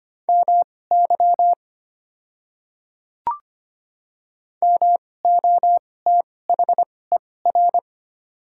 Morse code, heard as a keyed tone, about 25 wpm, spaced as for 18 wpm, 700 Hz. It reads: MY MOTHER
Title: 2x Character Spacing - Top 200 Words - 25wpm